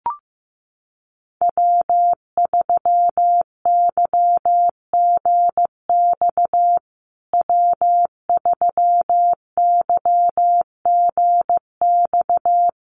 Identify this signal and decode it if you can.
Morse code, heard as a keyed tone, about 15 wpm, 700 Hz. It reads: W3YGX W3YGX
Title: USA Call Signs Repeated 3x 15wpm